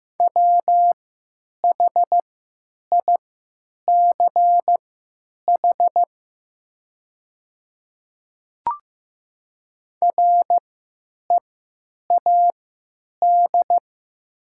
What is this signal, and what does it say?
Morse code, heard as a keyed tone, about 15 wpm, spaced as for 9 wpm, 700 Hz. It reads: WHICH READ
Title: 3x Character Spacing - Top 500 Words - 15wpm